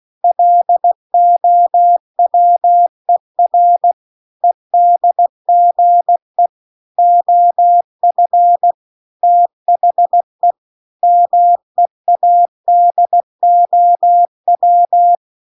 Morse code, heard as a keyed tone, about 16 wpm, 700 Hz. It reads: LOWER EDGE OF THE MEADOW